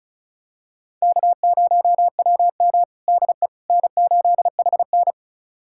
Morse code, heard as a keyed tone, about 36 wpm, 700 Hz. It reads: K0WM DE N8HN
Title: USA Callsign DE Callsign - 4 Characters - 35wpm